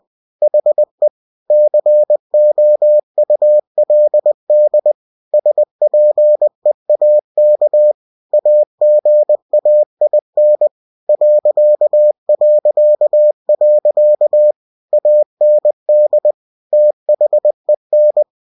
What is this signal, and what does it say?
Morse code, heard as a keyed tone, about 20 wpm, 600 Hz. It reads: HE COULD SPEAK AGAIN ... AND THEN
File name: cw-books / winnie-the-pooh / winnie-20wpm-page-0072